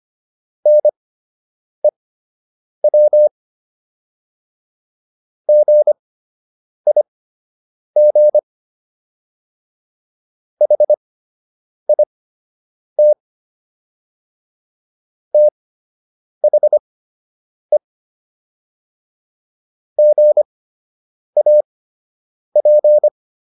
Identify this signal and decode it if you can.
Morse code, heard as a keyed tone, about 25 wpm, spaced as for 8 wpm, 600 Hz. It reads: NEW GIG HIT THE GAP